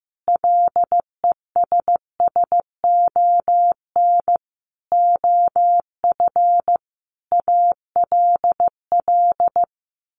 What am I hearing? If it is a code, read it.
Morse code, heard as a keyed tone, about 15 wpm, 700 Hz. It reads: LESSON OF ALL
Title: Sentences from Top 1200 Words 15wpm